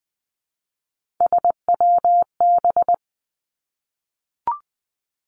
Morse code, heard as a keyed tone, about 20 wpm, 700 Hz. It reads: SWB
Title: Three Challenging Letter-Number Combinations 20wpm